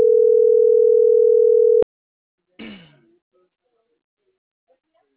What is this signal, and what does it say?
Ringback tone (North America), 440 + 480 Hz.